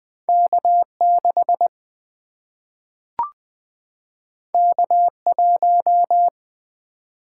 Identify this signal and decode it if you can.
Morse code, heard as a keyed tone, about 20 wpm, 700 Hz. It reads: K 6 K 1
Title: Single-Letter Single-Number 20wpm